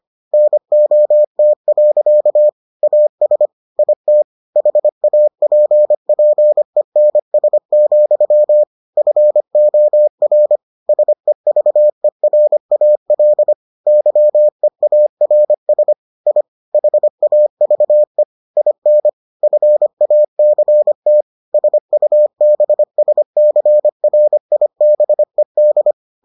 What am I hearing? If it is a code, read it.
Morse code, heard as a keyed tone, about 25 wpm, 600 Hz. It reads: NOT. AS IT HAPPENS, FOR SEVERAL YEARS I HAVE IN FACT SUBSCRIBED